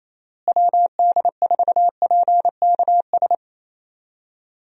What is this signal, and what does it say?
Morse code, heard as a keyed tone, about 27 wpm, 700 Hz. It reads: WD4PKS
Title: USA Call Signs - 6 Characters - Repeated 3x - 28wpm